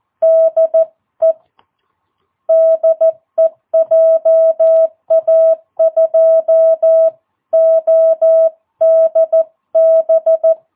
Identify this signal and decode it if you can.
Morse code, heard as a keyed tone, about 14 wpm, 660 Hz. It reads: DE DE JA2ODB
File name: jh8cbh / 2022cw